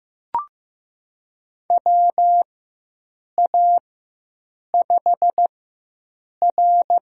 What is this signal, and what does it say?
Morse code, heard as a keyed tone, about 15 wpm, spaced as for 7 wpm, 700 Hz. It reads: WA5R
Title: USA Call Signs - 4x Character Spacing - 15wpm